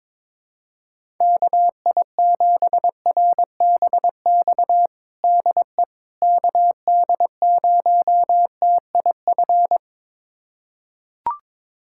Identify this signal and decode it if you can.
Morse code, heard as a keyed tone, about 22 wpm, 700 Hz. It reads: KI7RBX DE KD0TIF
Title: USA Callsign DE Callsign - 6 Characters - 22wpm